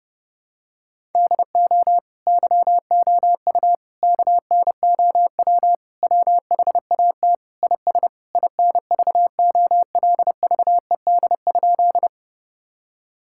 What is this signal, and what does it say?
Morse code, heard as a keyed tone, about 30 wpm, 700 Hz. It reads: DO YOU KNOW WHAT IS INVOLVED?
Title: Sentences from Top 1000 Words - Review Part 5 - 30wpm